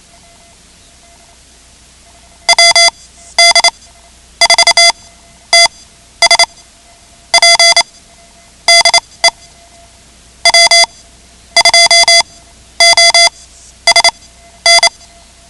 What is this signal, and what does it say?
Morse code, heard as a keyed tone, about 28 wpm, spaced as for 12 wpm, 2050 Hz. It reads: WD4TSP DE W2OSN
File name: ham radio / CodeQuick / Code Quick 10